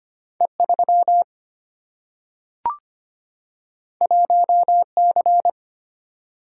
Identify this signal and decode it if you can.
Morse code, heard as a keyed tone, about 25 wpm, 700 Hz. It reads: E3 1C
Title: Single-Letter Single-Number 25wpm